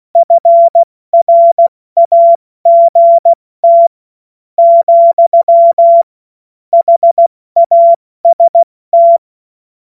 Morse code, heard as a keyed tone, about 16 wpm, spaced as for 13 wpm, 680 Hz. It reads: FRAGT, HAST
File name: Morsekurs / Witze / Witz 25. Mai 2021, 70 F80